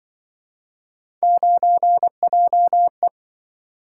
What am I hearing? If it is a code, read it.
Morse code, heard as a keyed tone, about 26 wpm, 700 Hz. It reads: 9JE